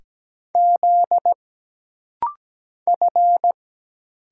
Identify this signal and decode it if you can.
Morse code, heard as a keyed tone, about 17 wpm, 700 Hz. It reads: Z F
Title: Single Letter-Number - Rapid-Fire 17wpm